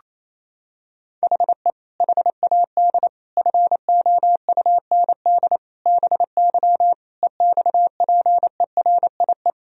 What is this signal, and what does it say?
Morse code, heard as a keyed tone, about 28 wpm, 700 Hz. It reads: HE HAD FOUND BY EXPERIE
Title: Sentences from Top 1000 Words - Review Part 3 - 28wpm